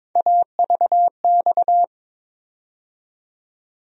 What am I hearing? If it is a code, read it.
Morse code, heard as a keyed tone, about 22 wpm, 700 Hz. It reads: AVX
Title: USA Call Sign Postfixes - 3 Letters 22wpm